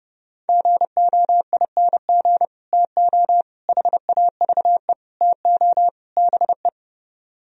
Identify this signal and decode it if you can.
Morse code, heard as a keyed tone, about 38 wpm, 700 Hz. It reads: GOING TO HAVE TO BE